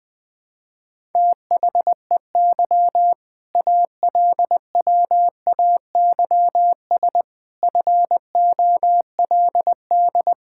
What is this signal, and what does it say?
Morse code, heard as a keyed tone, about 20 wpm, 700 Hz. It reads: THEY ALWAYS FOLD
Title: All-Age Appropriate Jokes - 20wpm